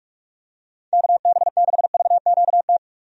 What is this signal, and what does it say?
Morse code, heard as a keyed tone, about 46 wpm, 700 Hz. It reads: KB6VQT
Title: USA Call Signs - 6 Characters - 45wpm